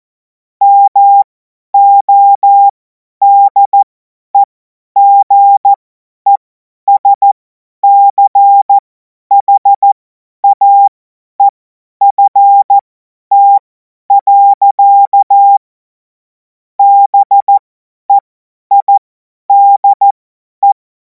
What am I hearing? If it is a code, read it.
Morse code, heard as a keyed tone, about 14 wpm, spaced as for 10 wpm, 800 Hz. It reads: MODEGESCHAEFT. BEIDE
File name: Morsekurs / Witze / Witz 17. Februar 2026 50 F70